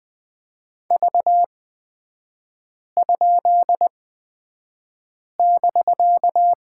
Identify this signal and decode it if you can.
Morse code, heard as a keyed tone, about 20 wpm, spaced as for 5 wpm, 700 Hz. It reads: V?<BK>
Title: 250 Lesson 40e - BK - Three Characters 20wpm